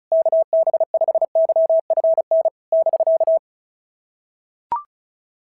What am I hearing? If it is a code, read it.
Morse code, heard as a keyed tone, about 36 wpm, 650 Hz. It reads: KB5YFN <BK>
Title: SOTA QSO Practice 35wpm